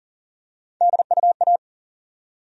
Morse code, heard as a keyed tone, about 40 wpm, 700 Hz. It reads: DUA
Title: USA Call Sign Postfixes - 3 Letters 40wpm